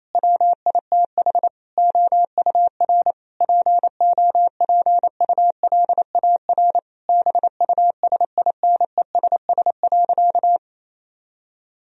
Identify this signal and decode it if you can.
Morse code, heard as a keyed tone, about 28 wpm, 700 Hz. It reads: WITH OUR POPULAR BUSINESS.
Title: Sentences from Top 1000 Words - Review Part 3 - 28wpm